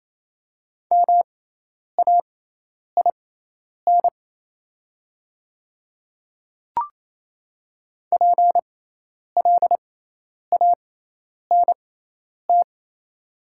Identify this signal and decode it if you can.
Morse code, heard as a keyed tone, about 28 wpm, spaced as for 10 wpm, 700 Hz. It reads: MAIN PLANT
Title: 6x Character Spacing - Top 300 Words - 28wpm